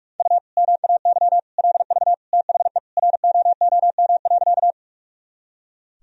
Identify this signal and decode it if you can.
Morse code, heard as a keyed tone, about 42 wpm, 700 Hz. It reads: U MAY LV THE ROOM.
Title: Sentences from Top 400 Words and Abbreviations 45wpm